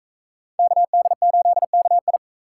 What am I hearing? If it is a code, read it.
Morse code, heard as a keyed tone, about 41 wpm, 700 Hz. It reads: KD8KI